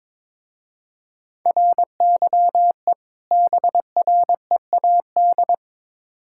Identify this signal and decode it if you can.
Morse code, heard as a keyed tone, about 22 wpm, 700 Hz. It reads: RYE BREAD
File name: Difficult Words with Hints - 22wpm